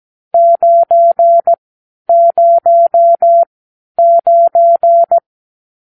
Morse code, heard as a keyed tone, about 17 wpm, spaced as for 10 wpm, 680 Hz. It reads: 909
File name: arrl / Review (1-0)